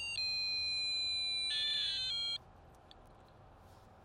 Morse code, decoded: none

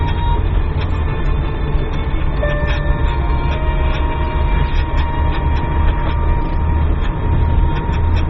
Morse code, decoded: none

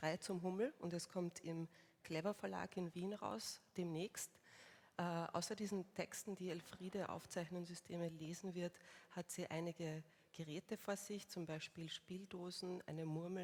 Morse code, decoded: none

